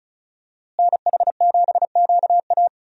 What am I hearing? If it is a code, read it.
Morse code, heard as a keyed tone, about 36 wpm, 700 Hz. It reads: NH7QA
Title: USA Call Signs - 5 Characters - Repeated 2x - 35wpm